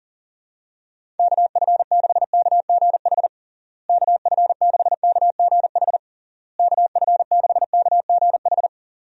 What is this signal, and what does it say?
Morse code, heard as a keyed tone, about 40 wpm, 700 Hz. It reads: KF6KGH KF6KGH KF6KGH
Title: USA Call Signs - 6 Characters - Repeated 3x - 40wpm